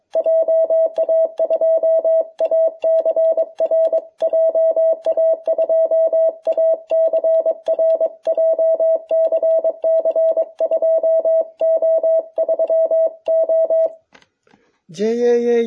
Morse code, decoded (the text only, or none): JA2ACR JA2ACR JCC 2O3O